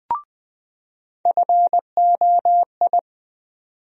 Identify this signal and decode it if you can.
Morse code, heard as a keyed tone, about 20 wpm, 700 Hz. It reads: FOI
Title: USA Call Sign Postfixes - 3 Letters 20wpm